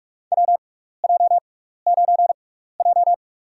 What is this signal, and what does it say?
Morse code, heard as a keyed tone, about 46 wpm, spaced as for 16 wpm, 700 Hz. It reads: WJ9J
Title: USA Call Signs - 4 Characters - 6x Character Spacing - 45wpm